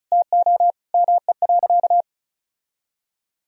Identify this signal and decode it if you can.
Morse code, heard as a keyed tone, about 36 wpm, 700 Hz. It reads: TO ME.
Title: Sentences from Top 1000 Words - Review Part 4 - 35wpm